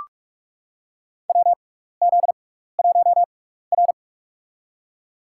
Morse code, heard as a keyed tone, about 46 wpm, spaced as for 15 wpm, 700 Hz. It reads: WZ1R
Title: USA Call Signs - 6x Character Spacing - 45wpm